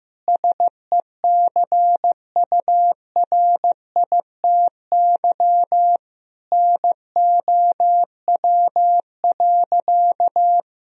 Morse code, heard as a keyed tone, about 16 wpm, 700 Hz. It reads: SECURITY NOW.
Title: Sentences from Top 1000 Words - Review Part 4 - 15wpm